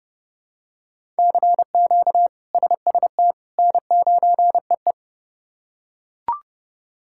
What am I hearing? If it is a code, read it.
Morse code, heard as a keyed tone, about 30 wpm, 700 Hz. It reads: CQ SST N9EE
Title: K1USN SST QSO Practice - 30wpm